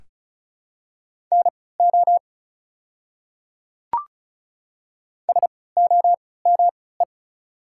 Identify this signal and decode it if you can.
Morse code, heard as a keyed tone, about 34 wpm, spaced as for 20 wpm, 690 Hz. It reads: NO SOME
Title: 3x Character Spacing - Top 100 Words - 35wpm